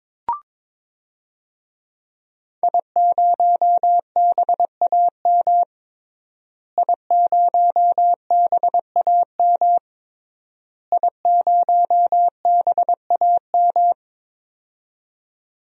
Morse code, heard as a keyed tone, about 21 wpm, 700 Hz. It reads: I0BAM I0BAM I0BAM
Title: DXCC-100 Call Signs - 5 Characters - Repeated 3x - 22wpm